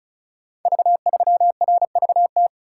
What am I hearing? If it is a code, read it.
Morse code, heard as a keyed tone, about 36 wpm, 700 Hz. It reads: V3RVT
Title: Code Groups - 5 Letter:Numbers - 35wpm